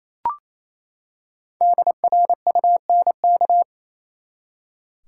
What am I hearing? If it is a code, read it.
Morse code, heard as a keyed tone, about 28 wpm, 700 Hz. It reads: DRUNK
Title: Five Letter Words - 28wpm